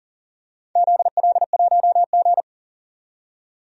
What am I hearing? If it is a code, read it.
Morse code, heard as a keyed tone, about 40 wpm, 700 Hz. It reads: ZL1G
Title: DXCC-100 Call Signs - 4 Characters - 40wpm